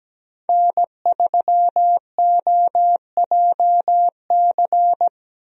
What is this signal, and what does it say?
Morse code, heard as a keyed tone, about 17 wpm, 700 Hz. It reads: N3OJC
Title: USA Call Signs - 5 Characters - Repeated 2x - 17wpm